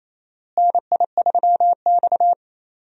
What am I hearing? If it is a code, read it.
Morse code, heard as a keyed tone, about 28 wpm, 700 Hz. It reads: NI3X
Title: USA Call Signs - 4 Characters - Repeated 2x - 28wpm